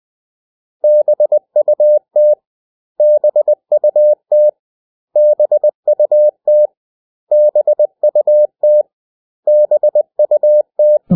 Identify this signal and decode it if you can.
Morse code, heard as a keyed tone, about 20 wpm, 600 Hz. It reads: BUT BUT BUT BUT BUT
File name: cw / 100 Common English Words